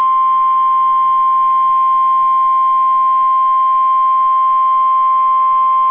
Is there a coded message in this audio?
Yes: Emergency Alert System (SAME) tones (the 1050 Hz attention tone) from 0.5 to 4.9 s.